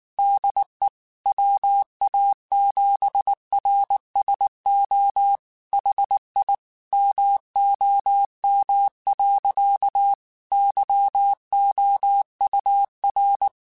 Morse code, heard as a keyed tone, about 19 wpm, 800 Hz. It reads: DE WA7RSO HI MOM. YOUR
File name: QSOs / QSOs-mp3s / WA7QOX 1@18